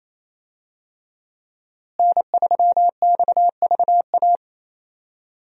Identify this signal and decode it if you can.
Morse code, heard as a keyed tone, about 28 wpm, 700 Hz. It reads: N3XVA
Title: USA Call Signs - 5 Characters - Repeated 3x - 28wpm